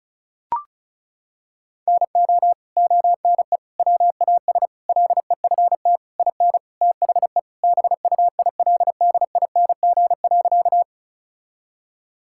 Morse code, heard as a keyed tone, about 34 wpm, 700 Hz. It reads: NO ONE WAS LEFT IN THE BUILDING.
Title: Sentences from Top 1000 Words - Review Part 5 - 35wpm